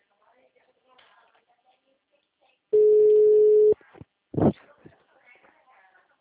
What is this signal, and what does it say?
Ringback tone (Europe, 425 Hz), with speech.